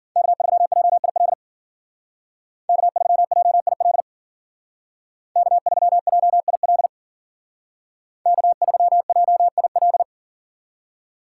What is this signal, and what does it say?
Morse code, heard as a keyed tone, about 52 wpm, 700 Hz. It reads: K3JIL K3JIL K3JIL K3JIL